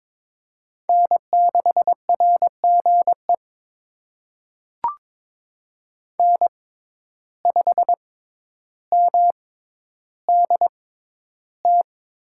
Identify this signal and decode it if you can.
Morse code, heard as a keyed tone, about 22 wpm, spaced as for 11 wpm, 700 Hz. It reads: N6RGE N5MDT
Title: USA Call Signs - 6x Character Spacing - 22wpm